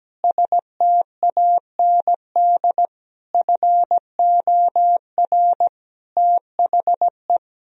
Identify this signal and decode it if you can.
Morse code, heard as a keyed tone, about 17 wpm, 700 Hz. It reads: STAND FOR THE
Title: Sentences from Top 1000 Words - Review Part 3 - 17wpm